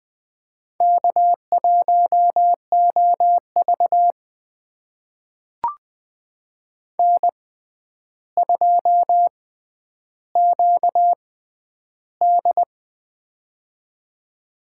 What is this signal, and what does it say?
Morse code, heard as a keyed tone, about 20 wpm, spaced as for 10 wpm, 700 Hz. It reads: K1OV N2QD
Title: USA Call Signs - 4 Characters - 6x Character Spacing - 20wpm